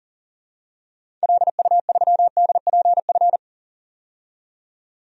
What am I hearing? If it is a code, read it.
Morse code, heard as a keyed tone, about 40 wpm, 700 Hz. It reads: LU3DPF
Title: DXCC-100 Call Signs - 6 Characters - 40wpm